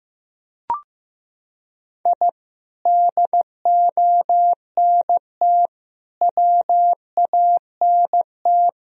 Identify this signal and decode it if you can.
Morse code, heard as a keyed tone, about 15 wpm, 700 Hz. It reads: I DONT WANT
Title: Sentences from Top 1000 Words - Review Part 1 - 15wpm